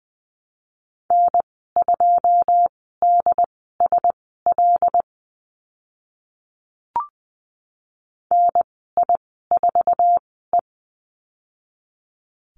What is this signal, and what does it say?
Morse code, heard as a keyed tone, about 20 wpm, spaced as for 14 wpm, 700 Hz. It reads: N2DSL NI4E
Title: 082 Lesson 14g - Number 2 - Call Signs 20wpm